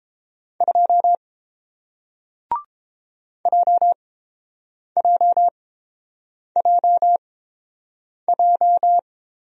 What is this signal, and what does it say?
Morse code, heard as a keyed tone, about 30 wpm, spaced as for 7 wpm, 700 Hz. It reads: 2 JJJJ